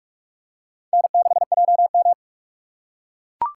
Morse code, heard as a keyed tone, about 41 wpm, 700 Hz. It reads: N6JM
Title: USA Call Signs - 6x Character Spacing - 45wpm